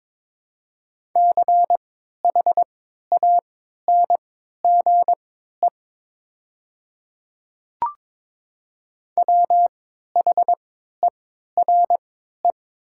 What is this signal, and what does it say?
Morse code, heard as a keyed tone, about 22 wpm, spaced as for 12 wpm, 700 Hz. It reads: CHANGE WHERE
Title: 3x Character Spacing - Top 300 Words - 22wpm